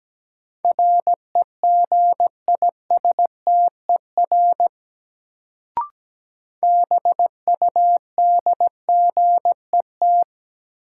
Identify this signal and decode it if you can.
Morse code, heard as a keyed tone, about 17 wpm, 700 Hz. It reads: REGISTER BUDGET
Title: Sentences from Top 1200 Words - New Word Drill - 17wpm